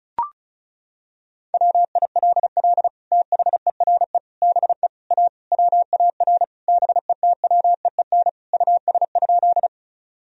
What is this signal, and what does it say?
Morse code, heard as a keyed tone, about 34 wpm, 700 Hz. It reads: WILL THERE BE A WAR BETWEEN US?
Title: Sentences from Top 1000 Words - Review Part 2 - 35wpm